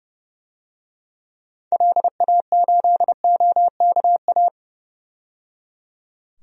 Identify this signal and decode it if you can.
Morse code, heard as a keyed tone, about 31 wpm, 700 Hz. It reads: LA8OKA
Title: DXCC-100 Call Signs - 6 Characters - 30wpm